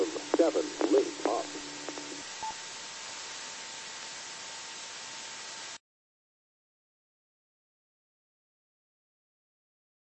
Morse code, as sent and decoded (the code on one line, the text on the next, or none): none